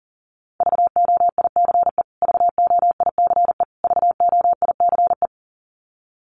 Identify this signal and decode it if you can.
Morse code, heard as a keyed tone, about 40 wpm, 700 Hz. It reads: VOICE VOICE VOICE
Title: Top 500 Words Repeated 40wpm